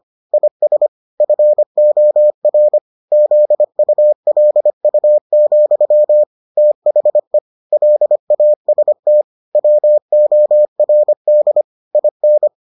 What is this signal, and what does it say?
Morse code, heard as a keyed tone, about 25 wpm, 600 Hz. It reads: IS FOR ZULU, THE LAST WORD IN